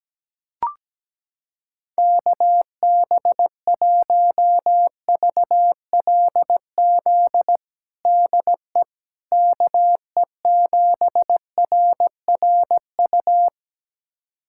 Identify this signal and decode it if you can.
Morse code, heard as a keyed tone, about 17 wpm, 700 Hz. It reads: KB1VLZ DE KE7RRU
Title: USA Callsign DE Callsign 17wpm